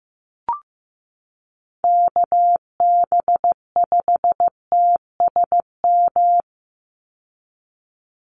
Morse code, heard as a keyed tone, about 15 wpm, 700 Hz. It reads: KB5TSM
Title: USA Call Signs 15wpm